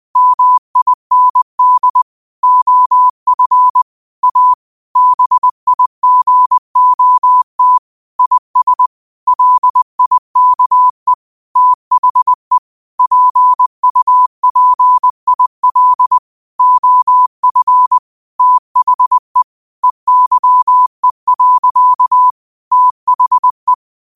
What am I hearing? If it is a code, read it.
Morse code, heard as a keyed tone, about 20 wpm, 1000 Hz. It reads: MIND OF A BIGOT IS LIKE THE PUPIL OF THE EYE. THE